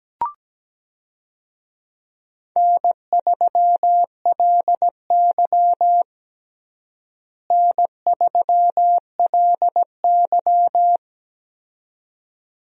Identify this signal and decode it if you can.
Morse code, heard as a keyed tone, about 17 wpm, 700 Hz. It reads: N3LY N3LY
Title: USA Call Signs - 4 Characters - Repeated 2x - 17wpm